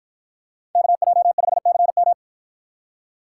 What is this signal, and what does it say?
Morse code, heard as a keyed tone, about 50 wpm, 700 Hz. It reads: KJ5CM